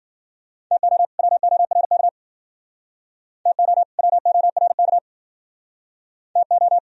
Morse code, heard as a keyed tone, about 53 wpm, 700 Hz. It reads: TO WORK TO WORK TO